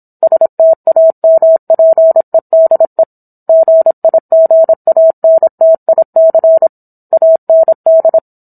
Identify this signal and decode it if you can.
Morse code, heard as a keyed tone, about 27 wpm, 650 Hz. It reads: STAMPEDE GIGANTIC AND
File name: Book One / 17-Chapter 17 - 26 WPM